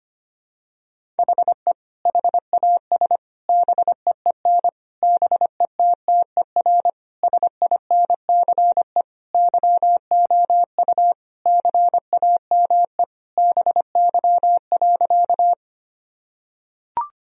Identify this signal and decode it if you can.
Morse code, heard as a keyed tone, about 25 wpm, 700 Hz. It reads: HE HAS BEEN BETTER SINCE YOU CAME BY.
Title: Sentences from Top 1000 Words - Review Part 1 - 25wpm